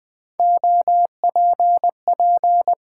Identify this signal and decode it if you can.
Morse code, heard as a keyed tone, about 20 wpm, 700 Hz. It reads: OPP
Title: Build A Word - 3 Letters At A Time - 6-Letter Words - 20wpm